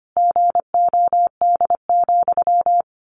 Morse code, heard as a keyed tone, about 25 wpm, 700 Hz. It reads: GOD,
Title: A Sound Of Thunder 25wpm